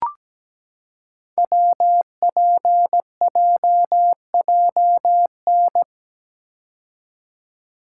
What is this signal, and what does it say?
Morse code, heard as a keyed tone, about 17 wpm, 700 Hz. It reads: WPJJN
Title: Code Groups - 5 Letters - 17wpm